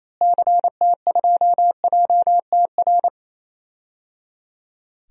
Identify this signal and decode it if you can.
Morse code, heard as a keyed tone, about 27 wpm, 700 Hz. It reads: CT2JTR
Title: DXCC-100 Call Signs - 6 Characters - Repeated 3x - 28wpm